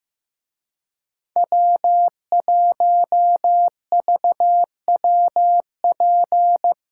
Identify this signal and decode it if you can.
Morse code, heard as a keyed tone, about 16 wpm, 700 Hz. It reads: W1VWP
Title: Speed Racing - USA 1x3 Call Signs 15wpm